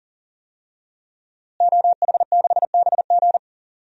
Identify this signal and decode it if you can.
Morse code, heard as a keyed tone, about 40 wpm, 700 Hz. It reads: OH6BG
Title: DXCC-100 Call Signs - 5 Characters - Repeated 2x - 40wpm